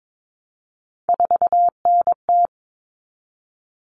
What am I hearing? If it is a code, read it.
Morse code, heard as a keyed tone, about 22 wpm, 700 Hz. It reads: 4NT
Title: Common Cut Numbers - 3-Digit 22wpm